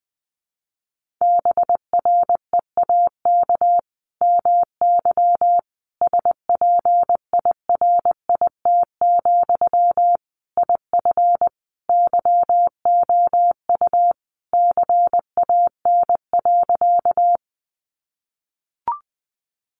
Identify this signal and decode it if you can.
Morse code, heard as a keyed tone, about 20 wpm, 700 Hz. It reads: BREAK MY SPIRIT, IF YOU CAN.
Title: Sentences from Top 1400 Words 20wpm